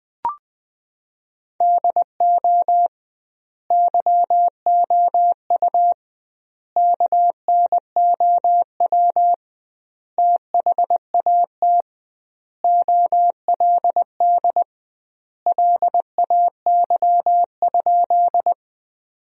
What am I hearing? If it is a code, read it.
Morse code, heard as a keyed tone, about 21 wpm, 700 Hz. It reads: DO YOU KNOW THAT OLD LAY?
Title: 2x Word-Spacing - Sentences from Top 400 Words - 20wpm